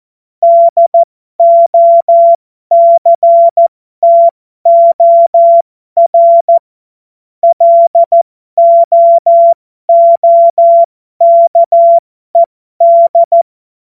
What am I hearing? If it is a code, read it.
Morse code, heard as a keyed tone, about 14 wpm, spaced as for 12 wpm, 680 Hz. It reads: DOCTOR LOOKED